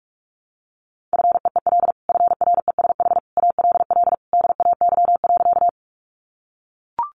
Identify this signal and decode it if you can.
Morse code, heard as a keyed tone, about 44 wpm, 700 Hz. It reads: FEEL FRESH ALL DAY.
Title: Sentences from Top 1200 Words 45wpm